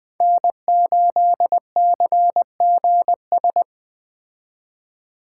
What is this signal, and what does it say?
Morse code, heard as a keyed tone, about 20 wpm, 700 Hz. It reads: N8CGS
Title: Speed Racing - USA Call Signs 17wpm